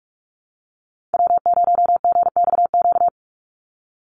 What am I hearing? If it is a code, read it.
Morse code, heard as a keyed tone, about 44 wpm, 700 Hz. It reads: W0GXQ
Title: Call Signs Repeated 3x - 45wpm